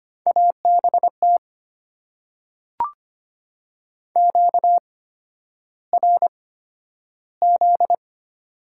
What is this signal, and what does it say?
Morse code, heard as a keyed tone, about 25 wpm, spaced as for 11 wpm, 700 Hz. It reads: ABT QRZ
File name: Top 90 QSO Elements - 8x Character Spacing - 25wpm